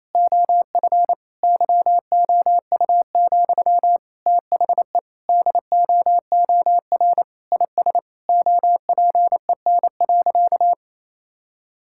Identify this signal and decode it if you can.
Morse code, heard as a keyed tone, about 27 wpm, 700 Hz. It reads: OF YOU, THE DOOR IS OPEN.
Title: Sentences from Top 1000 Words - Review Part 6 - 28wpm